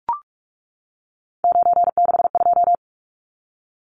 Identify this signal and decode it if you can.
Morse code, heard as a keyed tone, about 44 wpm, 700 Hz. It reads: 962
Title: Numbers - 3-Digit 45wpm